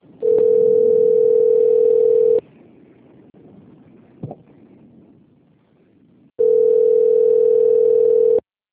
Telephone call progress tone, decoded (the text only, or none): ringback tone